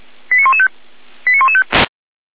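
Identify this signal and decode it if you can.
Five-tone selective call (ZVEI), heard as quick runs of five tones, tones of about 70 ms. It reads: 89115 89115
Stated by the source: Five-tone ZVEI selective call received by SDR, from the Feuerwehr Orlamünde fire brigade audio server.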